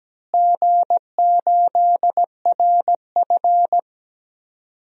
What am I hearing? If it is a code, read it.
Morse code, heard as a keyed tone, about 17 wpm, 700 Hz. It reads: G8RF
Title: DXCC-100 Call Signs - 4 Characters - Repeated 3x - 17wpm